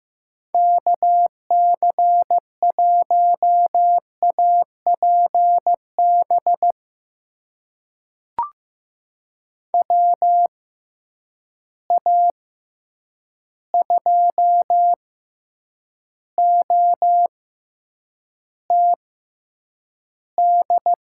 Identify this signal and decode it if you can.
Morse code, heard as a keyed tone, about 15 wpm, spaced as for 8 wpm, 700 Hz. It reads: KC1APB WA2OTD
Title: USA Call Signs - 6 Characters - 6x Character Spacing - 15wpm